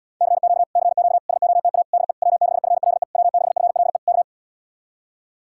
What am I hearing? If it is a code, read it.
Morse code, heard as a keyed tone, about 77 wpm, 700 Hz. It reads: CQ CQ SOTA DE W6FLE W6FLE K